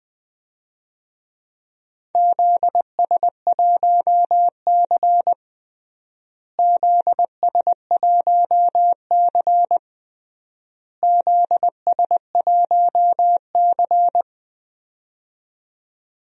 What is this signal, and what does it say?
Morse code, heard as a keyed tone, about 20 wpm, 700 Hz. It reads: ZS1C ZS1C ZS1C
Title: DXCC-100 Call Signs - 4 Characters - Repeated 3x - 20wpm